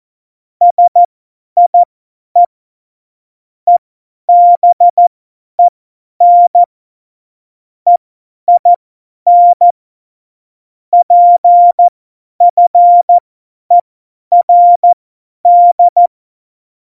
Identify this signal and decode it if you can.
Morse code, heard as a keyed tone, about 14 wpm, spaced as for 10 wpm, 700 Hz. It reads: SIE EBEN EIN PFERD
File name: Morsekurs / Witze / Witz 12 Mai 2020, 50 F70